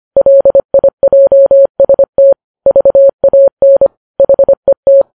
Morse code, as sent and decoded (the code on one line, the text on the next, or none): .-.. .. .--- ... - / ...- .- -. / .... -
LIJST VAN HET